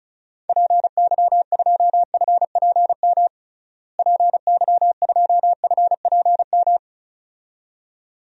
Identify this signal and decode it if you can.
Morse code, heard as a keyed tone, about 36 wpm, 700 Hz. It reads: PY2FPM PY2FPM
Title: DXCC-100 Call Signs - 6 Characters - Repeated 2x - 35wpm